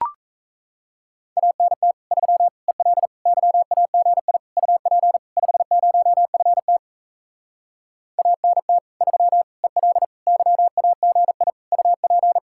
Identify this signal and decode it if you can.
Morse code, heard as a keyed tone, about 38 wpm, 700 Hz. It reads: ANT 3 EL YAGI UP 50FT ANT 3 EL YAGI UP